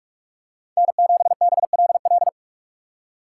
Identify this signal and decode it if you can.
Morse code, heard as a keyed tone, about 46 wpm, 700 Hz. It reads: N7BLL
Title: USA Call Signs - 5 Characters - 2x Character Spacing - 45wpm